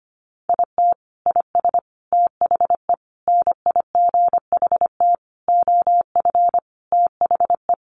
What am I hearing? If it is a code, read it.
Morse code, heard as a keyed tone, about 25 wpm, 700 Hz. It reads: IT IS THE NIGHT OF THE
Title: Sentences from Top 1400 Words 25wpm